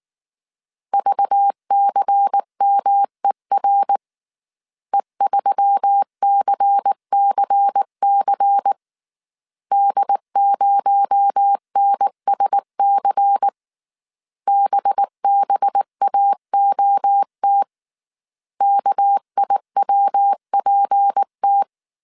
Morse code, heard as a keyed tone, about 19 wpm, 800 Hz. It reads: VCMER E3CCC D0NSC BBAOT KIWPT